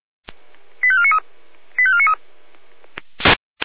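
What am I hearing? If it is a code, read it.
Five-tone selective call (ZVEI), heard as quick runs of five tones, tones of about 70 ms. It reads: 85493 85493
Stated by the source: ZVEI five-tone selective call received on an SDR.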